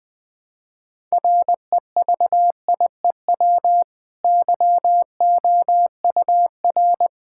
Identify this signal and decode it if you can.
Morse code, heard as a keyed tone, about 20 wpm, 700 Hz. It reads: REVIEW YOUR
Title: Easy Sentences with Hints - 20wpm